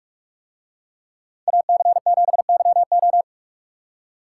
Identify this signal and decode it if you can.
Morse code, heard as a keyed tone, about 44 wpm, 700 Hz. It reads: AC7YO